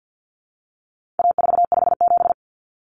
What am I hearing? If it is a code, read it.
Morse code, heard as a keyed tone, about 50 wpm, 700 Hz. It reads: A457
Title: Common Cut Numbers - 4-Digit 50wpm